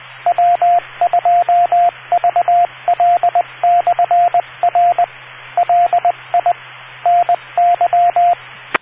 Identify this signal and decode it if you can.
Morse code, heard as a keyed tone, about 20 wpm, 700 Hz, with radio noise. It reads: W2VL/R LI NY